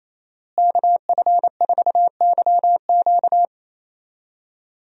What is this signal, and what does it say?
Morse code, heard as a keyed tone, about 28 wpm, 700 Hz. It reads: KF4YQ
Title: USA Call Signs - 5 Characters - Repeated 2x - 28wpm